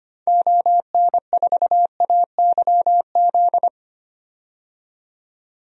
Morse code, heard as a keyed tone, about 25 wpm, 700 Hz. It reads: ON4AYZ
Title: DXCC-100 Call Signs - 6 Characters - 25wpm